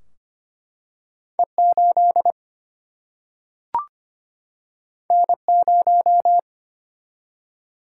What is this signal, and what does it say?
Morse code, heard as a keyed tone, about 25 wpm, 700 Hz. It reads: E8 N0
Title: Single-Letter Single-Number 25wpm